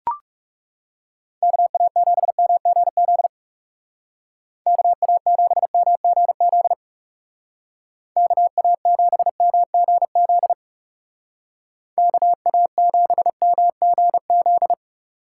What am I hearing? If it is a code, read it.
Morse code, heard as a keyed tone, about 35 wpm, 700 Hz. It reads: KA7MGZ KA7MGZ KA7MGZ KA7MGZ
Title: Speed Racing - USA 2x3 Call Signs 30wpm